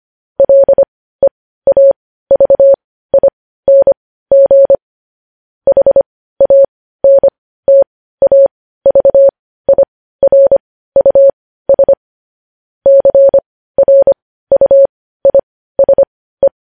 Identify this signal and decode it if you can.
Morse code, heard as a keyed tone, about 25 wpm, spaced as for 15 wpm, 550 Hz. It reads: LEAVING HANTAVIRUS CRUISE
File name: podcast / News headlines-1x-25x15